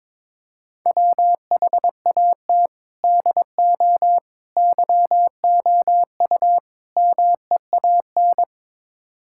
Morse code, heard as a keyed tone, about 22 wpm, 700 Hz. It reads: WHAT DO YOU MEAN
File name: practice / 22wpm / Build A Phrase - 5-Word Phrases - 22wpm